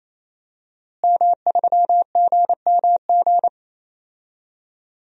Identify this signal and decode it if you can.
Morse code, heard as a keyed tone, about 27 wpm, 700 Hz. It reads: M3GMG
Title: DXCC-100 Call Signs - 5 Characters - Repeated 3x - 28wpm